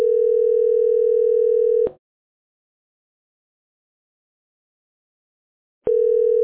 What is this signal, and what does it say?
Ringback tone (North America), 440 + 480 Hz.